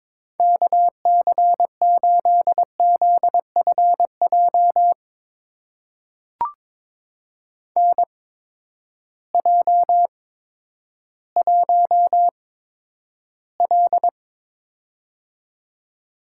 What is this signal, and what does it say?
Morse code, heard as a keyed tone, about 22 wpm, spaced as for 11 wpm, 700 Hz. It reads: KC8ZFJ NJ1L
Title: USA Call Signs - 8x Character Spacing - 22wpm